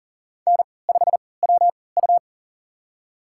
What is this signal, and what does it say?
Morse code, heard as a keyed tone, about 38 wpm, spaced as for 22 wpm, 700 Hz. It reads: N5WU
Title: USA Call Signs - 4 Characters - 3x Character Spacing - 40wpm